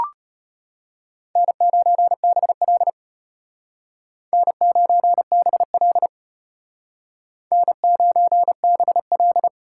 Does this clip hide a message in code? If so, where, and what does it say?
Yes: Morse code from 1.4 to 9.5 s, heard as a keyed tone, about 32 wpm, 700 Hz. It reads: N9BL N9BL N9BL